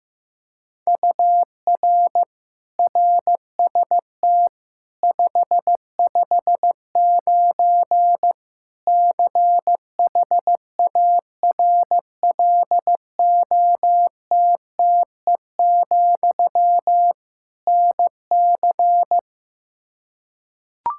Morse code, heard as a keyed tone, about 15 wpm, 700 Hz. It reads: UR RST 559 CHARLOTTE, NC
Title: Practice QSO - Exchange 1 - Beginning - 15wpm